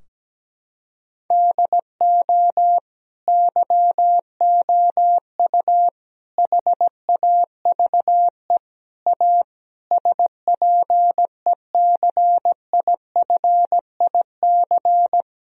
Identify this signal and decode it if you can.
Morse code, heard as a keyed tone, about 17 wpm, 700 Hz. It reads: DO YOU HAVE A SPECIFIC
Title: Sentences from Top 1000 Words - Review Part 3 - 17wpm